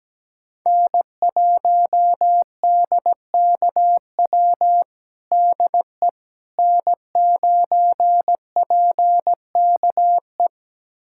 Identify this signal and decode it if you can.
Morse code, heard as a keyed tone, about 17 wpm, 700 Hz. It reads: N1DKW DE N9PKE
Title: USA Callsign DE Callsign - 5 Characters - 17wpm